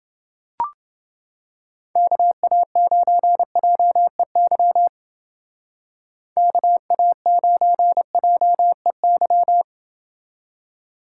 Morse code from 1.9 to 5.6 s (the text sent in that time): KA9JEY